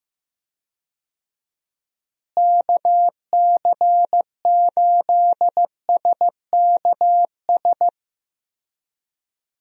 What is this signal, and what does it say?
Morse code, heard as a keyed tone, about 15 wpm, 700 Hz. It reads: KC8SKS
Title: USA Call Signs - 6 Characters - Repeated 3x - 15wpm